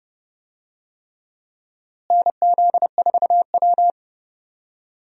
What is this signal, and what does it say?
Morse code, heard as a keyed tone, about 32 wpm, 700 Hz. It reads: NZ4W
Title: USA Call Signs - 4 Characters - Repeated 3x - 30wpm